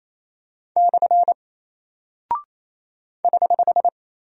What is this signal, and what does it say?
Morse code, heard as a keyed tone, about 28 wpm, 700 Hz. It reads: / <HH>
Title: Single Character (Letter:Number:Punctuation:Prosign) Alt 28wpm